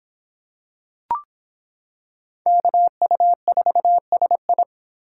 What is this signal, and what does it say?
Morse code, heard as a keyed tone, about 25 wpm, 700 Hz. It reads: KU4SI